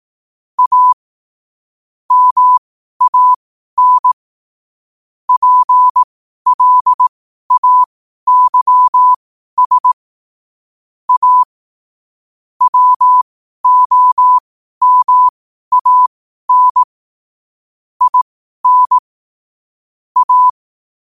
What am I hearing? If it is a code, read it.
Morse code, heard as a keyed tone, about 18 wpm, spaced as for 12 wpm, 1000 Hz. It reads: A MAN PLAYS A WOMAN IN A
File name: qotdfiles / 12wpm